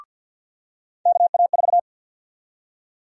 Morse code, heard as a keyed tone, about 51 wpm, 700 Hz. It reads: KA4
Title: USA Call Sign Prefixes - 50wpm